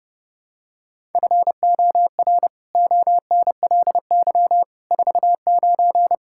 Morse code, heard as a keyed tone, about 30 wpm, 700 Hz. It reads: FOR ONLY 49
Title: Pangram Challenge v2 - 30wpm